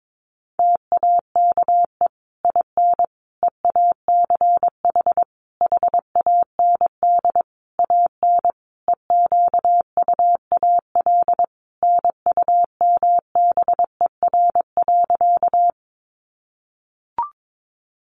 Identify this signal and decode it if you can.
Morse code, heard as a keyed tone, about 22 wpm, 700 Hz. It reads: TAKE IN EACH HAND AN EQUAL NUMBER.
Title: Sentences from Top 1400 Words 22wpm